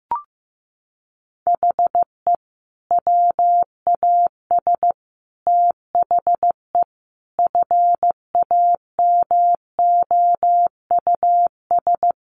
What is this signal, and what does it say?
Morse code, heard as a keyed tone, about 15 wpm, 700 Hz. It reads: HE WAS THE FAMOUS
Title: Sentences from Top 1500 Words 15wpm